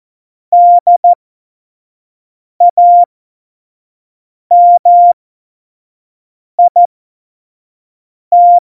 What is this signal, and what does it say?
Morse code, heard as a keyed tone, about 14 wpm, spaced as for 5 wpm, 700 Hz. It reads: DAMIT
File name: Morsekurs / Witze / Witz 27. Oktober 2020, 25 F70